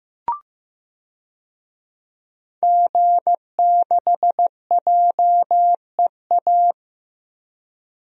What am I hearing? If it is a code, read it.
Morse code, heard as a keyed tone, about 15 wpm, 700 Hz. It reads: G6JEA
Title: DXCC-100 Call Signs - 5 Characters - Repeated 3x - 15wpm